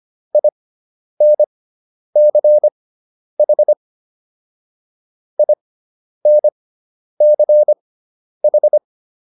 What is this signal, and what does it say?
Morse code, heard as a keyed tone, about 25 wpm, spaced as for 10 wpm, 600 Hz. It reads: INCH INCH